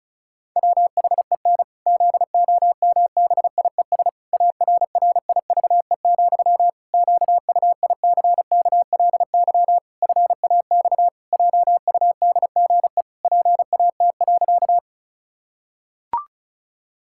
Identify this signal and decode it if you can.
Morse code, heard as a keyed tone, about 34 wpm, 700 Hz. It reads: WHEN ZOMBIES ARRIVE, QUICKLY FAX JUDGE PAT.
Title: Pangram Challenge v2 - 35wpm